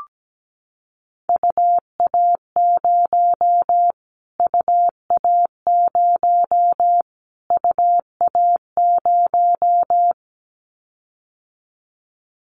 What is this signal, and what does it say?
Morse code, heard as a keyed tone, about 17 wpm, 700 Hz. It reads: UA0 UA0 UA0
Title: Callsign Prefixes for Top 25 DXCC - Repeated 3x - 17wpm